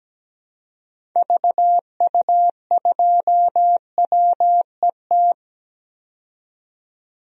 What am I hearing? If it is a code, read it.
Morse code, heard as a keyed tone, about 17 wpm, 700 Hz. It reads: VU2WET
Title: DXCC-100 Call Signs - 6 Characters - 17wpm